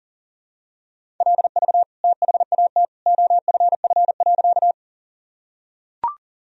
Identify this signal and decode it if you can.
Morse code, heard as a keyed tone, about 40 wpm, 700 Hz. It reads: LV THAT OFF.
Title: Sentences from Top 400 Words and Abbreviations 40wpm